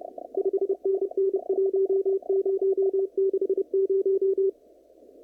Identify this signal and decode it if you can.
Morse code, heard as a keyed tone, about 29 wpm, 380 Hz. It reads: 5NN 1060
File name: CW / kt5j wpx cw-op-k5pi